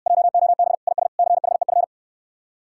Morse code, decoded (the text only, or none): JOB IS CHEF